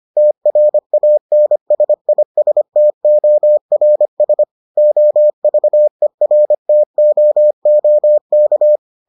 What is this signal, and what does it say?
Morse code, heard as a keyed tone, about 25 wpm, 600 Hz. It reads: TRANSISTORS OVERTOOK